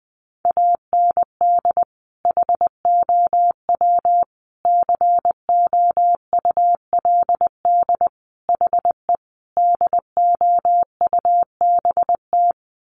Morse code, heard as a keyed tone, about 20 wpm, 700 Hz. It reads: AND HOW COULD HE DOUBT
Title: Sentences from Top 1500 Words 20wpm